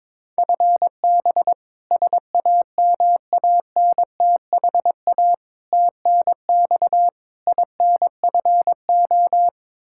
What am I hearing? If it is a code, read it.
Morse code, heard as a keyed tone, about 22 wpm, 700 Hz. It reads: FB SAMANTHA TNX INFO
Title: K1USN SST QSO Practice 22wpm